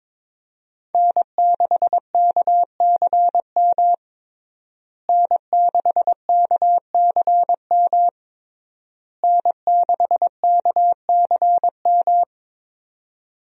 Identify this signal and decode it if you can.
Morse code, heard as a keyed tone, about 22 wpm, 700 Hz. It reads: N6KCM N6KCM N6KCM
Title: USA Call Signs - 5 Characters - Repeated 3x - 22wpm